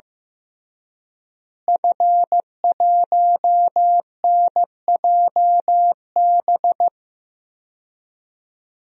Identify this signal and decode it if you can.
Morse code, heard as a keyed tone, about 15 wpm, 700 Hz. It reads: F1NJB
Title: DXCC-100 Call Signs - 5 Characters - Repeated 2x - 15wpm